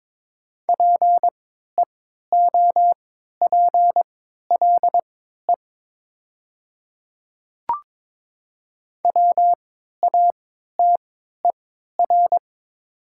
Morse code, heard as a keyed tone, about 22 wpm, spaced as for 12 wpm, 700 Hz. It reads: PEOPLE WATER
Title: 3x Character Spacing - Top 300 Words - 22wpm